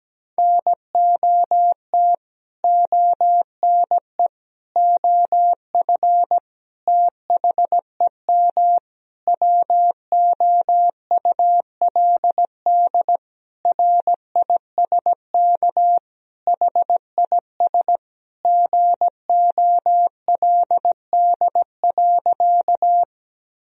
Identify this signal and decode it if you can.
Morse code, heard as a keyed tone, about 17 wpm, 700 Hz. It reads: NOT ONE OF THEM WOULD RISK HIS GOLD.
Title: Sentences from Top 1000 Words - Review Part 4 - 17wpm